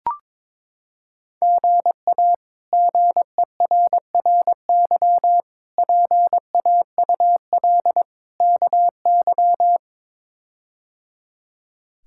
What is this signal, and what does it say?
Morse code, heard as a keyed tone, about 22 wpm, 700 Hz. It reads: GA GERRY PAUL KY
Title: K1USN SST QSO Practice - 22wpm